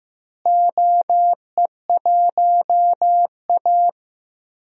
Morse code, heard as a keyed tone, about 15 wpm, 700 Hz. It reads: OE1A
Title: DXCC-100 Call Signs - 4 Characters - Repeated 2x - 15wpm